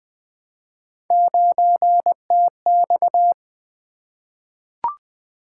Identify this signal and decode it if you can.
Morse code, heard as a keyed tone, about 20 wpm, 700 Hz. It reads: 9TX
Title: Three Letter-Number Combinations - 20wpm